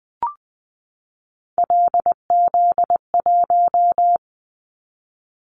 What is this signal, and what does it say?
Morse code, heard as a keyed tone, about 19 wpm, 700 Hz. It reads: LZ1
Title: Callsign Prefixes for Top 100 DXCC - 20wpm